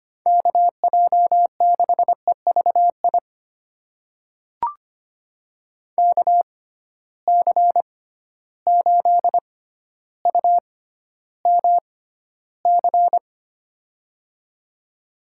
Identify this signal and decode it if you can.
Morse code, heard as a keyed tone, about 25 wpm, spaced as for 13 wpm, 700 Hz. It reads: KJ6EVI KC8UMC
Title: USA Call Signs - 6 Characters - 6x Character Spacing - 25wpm